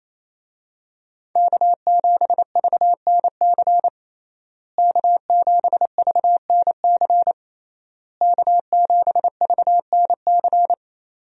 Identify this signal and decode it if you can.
Morse code, heard as a keyed tone, about 28 wpm, 700 Hz. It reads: K7VNC K7VNC K7VNC
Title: USA Call Signs - 5 Characters - Repeated 3x - 28wpm